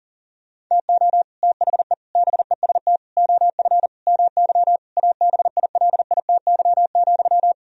Morse code, heard as a keyed tone, about 40 wpm, 700 Hz. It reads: TO THE BEST OF MY ABILITY,
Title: Sentences from Top 1000 Words - Review Part 3 - 40wpm